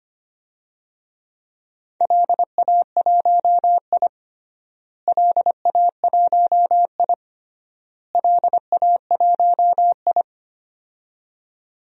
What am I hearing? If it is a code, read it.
Morse code, heard as a keyed tone, about 25 wpm, 700 Hz. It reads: LA1I LA1I LA1I
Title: DXCC-100 Call Signs - 4 Characters - Repeated 3x - 25wpm